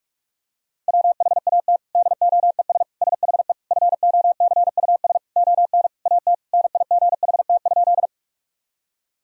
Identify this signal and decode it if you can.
Morse code, heard as a keyed tone, about 41 wpm, 700 Hz. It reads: WHAT DOES SHE FOCUS ON AT NIGHT?
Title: Sentences from Top 1000 Words - Review Part 4 - 45wpm